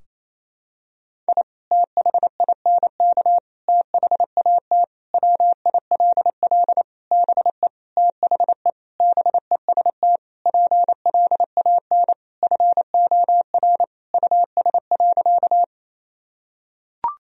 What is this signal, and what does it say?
Morse code, heard as a keyed tone, about 28 wpm, 700 Hz. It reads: I THINK THAT WILL BE THE BEST PLAN FOR US.
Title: Sentences from Top 1000 Words - Review Part 2 - 28wpm